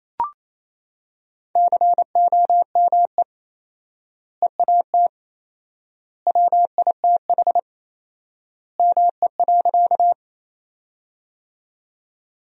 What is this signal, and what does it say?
Morse code, heard as a keyed tone, about 28 wpm, 700 Hz. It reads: COME EAT WITH ME.
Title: 4x Word-Spacing - Sentences from Top 300 Words - 28wpm